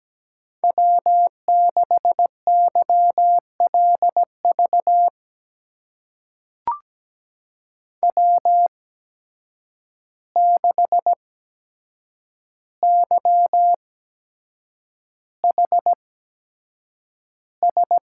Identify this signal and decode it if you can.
Morse code, heard as a keyed tone, about 17 wpm, spaced as for 7 wpm, 700 Hz. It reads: W6YLV W6YHS
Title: USA Call Signs - 5 Characters - 8x Character Spacing - 17wpm